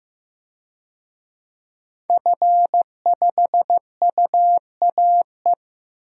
Morse code, heard as a keyed tone, about 15 wpm, 700 Hz. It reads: F5UAE
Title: DXCC-100 Call Signs - 5 Characters - Repeated 3x - 15wpm